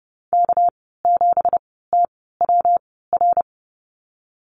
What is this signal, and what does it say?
Morse code, heard as a keyed tone, about 30 wpm, spaced as for 17 wpm, 700 Hz. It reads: K7TWR
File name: Lesson 37g - Letter x - Call Signs 30wpm